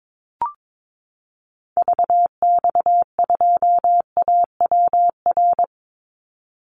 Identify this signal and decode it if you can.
Morse code, heard as a keyed tone, about 22 wpm, 700 Hz. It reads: VX2AWR
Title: Call Signs Repeated 2x - 22wpm